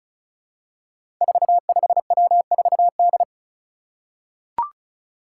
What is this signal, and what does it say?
Morse code, heard as a keyed tone, about 36 wpm, 700 Hz. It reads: 45W4D
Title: Five Challenging Letter-Number Combinations 35wpm